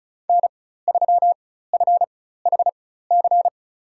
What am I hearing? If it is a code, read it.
Morse code, heard as a keyed tone, about 34 wpm, spaced as for 16 wpm, 700 Hz. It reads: N3FHC
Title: USA Call Signs - 5 Characters - 4x Character Spacing - 35wpm